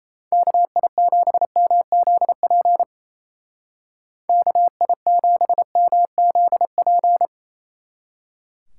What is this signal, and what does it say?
Morse code, heard as a keyed tone, about 35 wpm, 700 Hz. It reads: KI7MZP KI7MZP